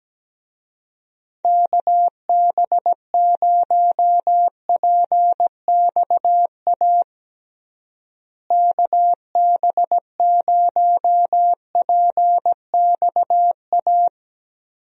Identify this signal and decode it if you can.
Morse code, heard as a keyed tone, about 17 wpm, 700 Hz. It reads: KB0PXA KB0PXA
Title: USA Call Signs - 6 Characters - Repeated 3x - 17wpm